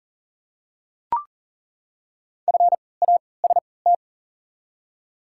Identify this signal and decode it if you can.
Morse code, heard as a keyed tone, about 40 wpm, spaced as for 23 wpm, 690 Hz. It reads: FAST
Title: 3x Character Spacing - Top 500 Words - 40wpm